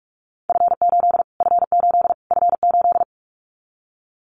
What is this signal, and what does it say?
Morse code, heard as a keyed tone, about 44 wpm, 700 Hz. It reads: F8 F8 F8
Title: Callsign Prefixes for Top 25 DXCC - Repeated 3x - 45wpm